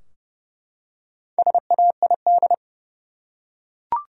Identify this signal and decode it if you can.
Morse code, heard as a keyed tone, about 30 wpm, 700 Hz. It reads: SAID
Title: Top 100 Words Repeated 30wpm